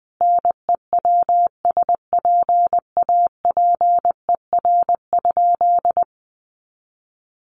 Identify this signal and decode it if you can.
Morse code, heard as a keyed tone, about 20 wpm, 700 Hz. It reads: NEWSPAPER?
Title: Sentences from Top 1100 Words 20wpm